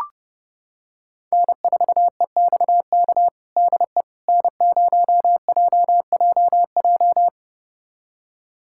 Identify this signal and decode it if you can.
Morse code, heard as a keyed tone, about 30 wpm, 700 Hz. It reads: N4EXK DE N0JJJ
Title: USA Callsign DE Callsign - 5 Characters - 30wpm